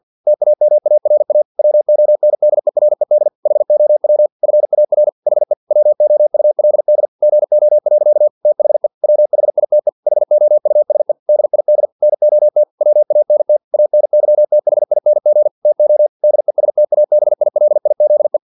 Morse code, heard as a keyed tone, about 50 wpm, 600 Hz. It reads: TAMARA WONDERED HOW FAR HE WOULD GO. THE WHITE HOUSE DID NOT WANT ANYTHING TO DESTABILIZE
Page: Never 49wpm